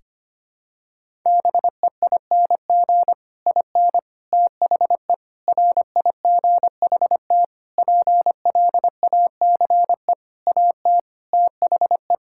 Morse code, heard as a keyed tone, about 25 wpm, 700 Hz. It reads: BEING IN THE RIGHT PLACE AT THE
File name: Sets of 9 Words - 25wpm